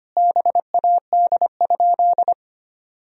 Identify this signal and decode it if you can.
Morse code, heard as a keyed tone, about 25 wpm, 700 Hz. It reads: BAD?
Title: Sentences from Top 1000 Words - Review Part 1 - 25wpm